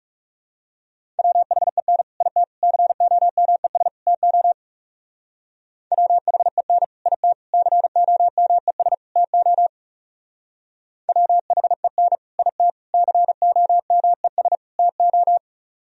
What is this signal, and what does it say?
Morse code, heard as a keyed tone, about 38 wpm, 700 Hz. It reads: WHEN IT COMES TO WHEN IT COMES TO WHEN IT COMES TO